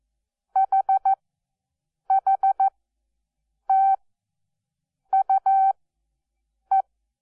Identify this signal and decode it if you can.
Morse code, heard as a keyed tone, about 14 wpm, spaced as for 7 wpm, 780 Hz. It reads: HHTUE
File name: Onlinelehrgang / morsen / DJ4UF Morsen12